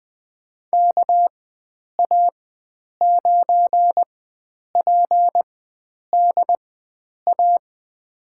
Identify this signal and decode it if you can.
Morse code, heard as a keyed tone, about 20 wpm, spaced as for 9 wpm, 700 Hz. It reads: KA9PDA
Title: USA Call Signs - 4x Character Spacing - 20wpm